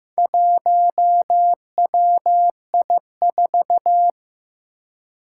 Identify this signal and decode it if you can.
Morse code, heard as a keyed tone, about 15 wpm, 700 Hz. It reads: 1WI4
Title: Four Challenging Letter-Number Combinations 15wpm